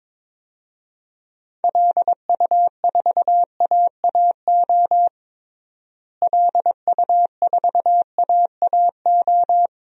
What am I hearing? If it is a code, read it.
Morse code, heard as a keyed tone, about 22 wpm, 700 Hz. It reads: LU4AAO LU4AAO
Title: DXCC-100 Call Signs - 6 Characters - Repeated 3x - 22wpm